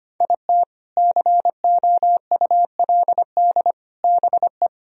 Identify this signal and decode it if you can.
Morse code, heard as a keyed tone, about 33 wpm, 700 Hz. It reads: IT COULD BE